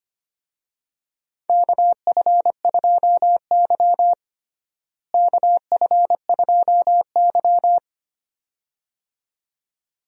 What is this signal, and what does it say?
Morse code, heard as a keyed tone, about 25 wpm, 700 Hz. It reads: KF2Y KF2Y
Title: USA Call Signs - 4 Characters - Repeated 2x - 25wpm